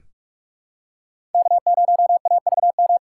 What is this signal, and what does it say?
Morse code, heard as a keyed tone, about 46 wpm, 700 Hz. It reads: K0AVM
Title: USA Call Signs - 5 Characters - 2x Character Spacing - 45wpm